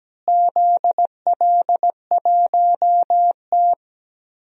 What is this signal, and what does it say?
Morse code, heard as a keyed tone, about 17 wpm, 700 Hz. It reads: ZL1T
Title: DXCC-100 Call Signs - 4 Characters - Repeated 2x - 17wpm